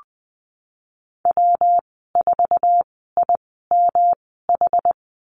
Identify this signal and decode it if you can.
Morse code, heard as a keyed tone, about 19 wpm, spaced as for 14 wpm, 700 Hz. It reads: W4IMH
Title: 122 Lesson 20g - Number 3 - Call Signs 20wpm